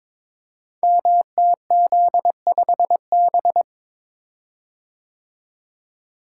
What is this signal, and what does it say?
Morse code, heard as a keyed tone, about 22 wpm, 700 Hz. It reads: MTZ5B
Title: Code Groups - 5 Letter:Numbers - 22wpm